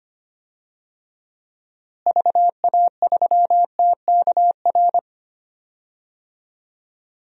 Morse code, heard as a keyed tone, about 25 wpm, 700 Hz. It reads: VA3TKR
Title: DXCC-100 Call Signs - 6 Characters - Repeated 2x - 25wpm